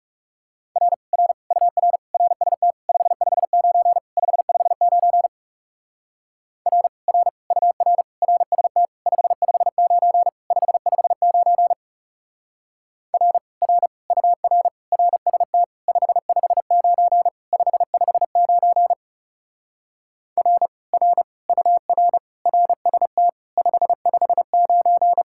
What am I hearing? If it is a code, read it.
Morse code, heard as a keyed tone, about 36 wpm, 700 Hz. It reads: R R UR RST 559 559 R R UR RST 559 559 R R UR RST 559 559 R R UR RST 559
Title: Speed-Racing Common QSO Phrases Encore - 30wpm